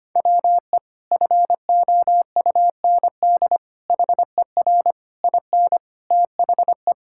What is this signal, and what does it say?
Morse code, heard as a keyed tone, about 25 wpm, 700 Hz. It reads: WE FOUND HER IN THE
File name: practice / 25wpm / Sentences with 4-Word Phrases - 25wpm